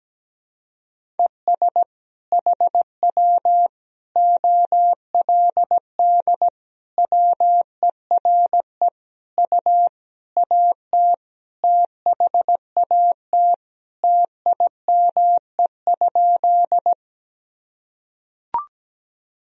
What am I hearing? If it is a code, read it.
Morse code, heard as a keyed tone, about 17 wpm, 700 Hz. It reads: ES HW OLD WERE U AT THAT TIME?
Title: Sentences from Top 200 Words and Abbreviations 17wpm